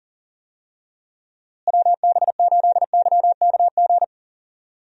Morse code, heard as a keyed tone, about 40 wpm, 700 Hz. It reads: WB8YKG
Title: USA Call Signs - 6 Characters - Repeated 2x - 40wpm